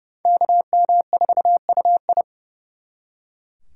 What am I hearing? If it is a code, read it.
Morse code, heard as a keyed tone, about 37 wpm, 700 Hz. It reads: KM4UI
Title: Speed Racing - USA 2x2 Call Signs 30wpm